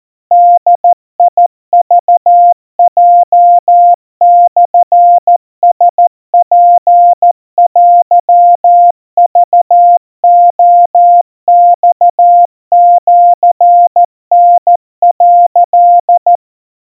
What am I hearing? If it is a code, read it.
Morse code (Japanese Wabun code), heard as a keyed tone, about 14 wpm, 700 Hz. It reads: ボクヲモラツテクレマシタ」
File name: kamyu 50 cpm 700 Hz Crispy